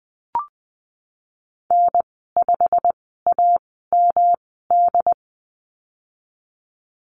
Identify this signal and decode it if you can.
Morse code, heard as a keyed tone, about 20 wpm, spaced as for 15 wpm, 700 Hz. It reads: N5AMD
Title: 122 Lesson 20g - Number 3 - Call Signs 20wpm